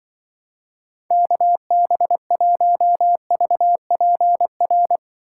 Morse code, heard as a keyed tone, about 24 wpm, 700 Hz. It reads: KB1VPR
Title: Speed Racing - USA 2x3 Call Signs 20wpm